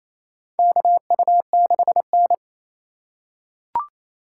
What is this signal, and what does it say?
Morse code, heard as a keyed tone, about 28 wpm, 700 Hz. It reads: KU6N
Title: USA Call Signs - 4 Characters - 2x Character Spacing - 28wpm